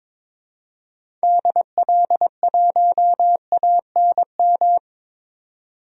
Morse code, heard as a keyed tone, about 21 wpm, 700 Hz. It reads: DL1ANM
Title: DXCC-100 Call Signs - 6 Characters - 22wpm